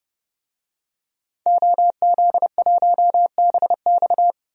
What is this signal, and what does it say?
Morse code, heard as a keyed tone, about 31 wpm, 700 Hz. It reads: OZ1BX
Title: DXCC-100 Call Signs - 5 Characters - 30wpm